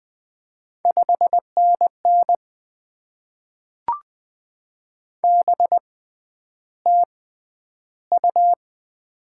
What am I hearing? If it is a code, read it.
Morse code, heard as a keyed tone, about 20 wpm, spaced as for 10 wpm, 700 Hz. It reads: 5NN BTU
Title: Top 20 QSO Elements - 6x Character Spacing - 20wpm